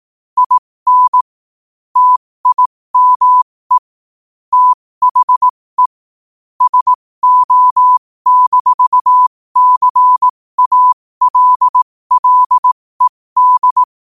Morse code, heard as a keyed tone, about 18 wpm, spaced as for 15 wpm, 1000 Hz. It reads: IN TIME THE SO-CALLED
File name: qotdfiles / 15wpm